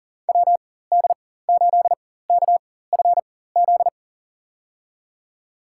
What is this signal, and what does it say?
Morse code, heard as a keyed tone, about 40 wpm, spaced as for 19 wpm, 700 Hz. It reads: WD8KFZ
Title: USA Call Signs - 6 Characters - 4x Character Spacing - 40wpm